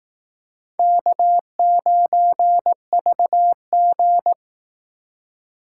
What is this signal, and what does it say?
Morse code, heard as a keyed tone, about 18 wpm, 700 Hz. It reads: K9VG
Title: Speed Racing - USA 1x2 Call Signs 15wpm